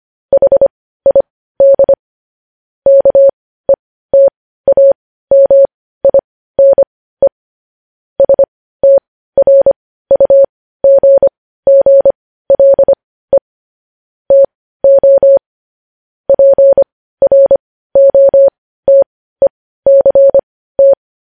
Morse code, heard as a keyed tone, about 25 wpm, spaced as for 15 wpm, 550 Hz. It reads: HID KETAMINE STRUGGLE TO PROTECT
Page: News headlines in morse code at 15 wpm